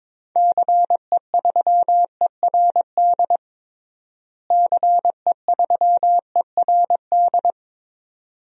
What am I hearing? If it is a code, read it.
Morse code, heard as a keyed tone, about 22 wpm, 700 Hz. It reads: CE3ERD CE3ERD
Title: DXCC-100 Call Signs - 6 Characters - Repeated 2x - 22wpm